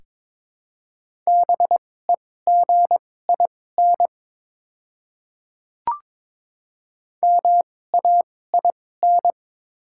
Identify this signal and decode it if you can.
Morse code, heard as a keyed tone, about 22 wpm, spaced as for 16 wpm, 700 Hz. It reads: BEGIN MAIN
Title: 2x Character Spacing - Top 300 Words - 22wpm